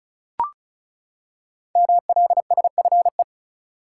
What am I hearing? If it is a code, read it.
Morse code, heard as a keyed tone, about 36 wpm, 700 Hz. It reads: MLSFE
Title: Code Groups - 5 Letters - 35wpm